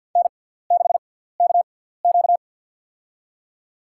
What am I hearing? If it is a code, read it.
Morse code, heard as a keyed tone, about 54 wpm, spaced as for 17 wpm, 690 Hz. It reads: N6KQ
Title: USA Call Signs - 4 Characters - 6x Character Spacing - 50wpm